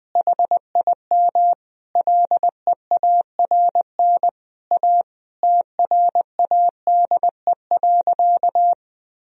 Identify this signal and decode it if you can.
Morse code, heard as a keyed tone, about 21 wpm, 700 Hz. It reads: HIM LEARN A TRADE.
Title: Sentences from Top 1000 Words - Review Part 4 - 20wpm